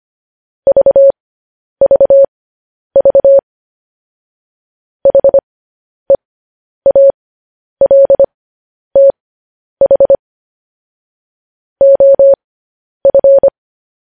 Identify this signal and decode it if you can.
Morse code, heard as a keyed tone, about 25 wpm, spaced as for 10 wpm, 550 Hz. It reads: VVV HEALTH OF
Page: news headlines in morse code at 10 wpm